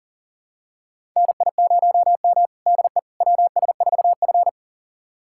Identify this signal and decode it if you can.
Morse code, heard as a keyed tone, about 40 wpm, 700 Hz. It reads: NI0M DE WS4F
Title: USA Callsign DE Callsign - 4 Characters - 40wpm